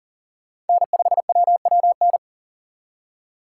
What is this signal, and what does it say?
Morse code, heard as a keyed tone, about 49 wpm, 700 Hz. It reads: N5WWN
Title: Speed Racing - USA Call Signs 40wpm